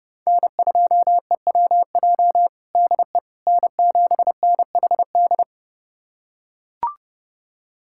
Morse code, heard as a keyed tone, about 30 wpm, 700 Hz. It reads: N2EWJ DE N7NHD
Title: USA Callsign DE Callsign - 5 Characters - 30wpm